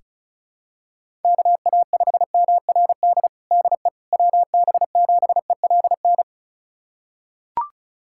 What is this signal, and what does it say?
Morse code, heard as a keyed tone, about 36 wpm, 700 Hz. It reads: KA5MRD DE WB7ELN
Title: USA Callsign DE Callsign 35wpm